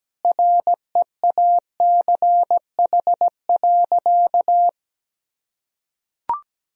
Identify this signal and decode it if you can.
Morse code, heard as a keyed tone, about 17 wpm, 700 Hz. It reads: REACH.
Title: Sentences from Top 1000 Words - Review Part 2 - 17wpm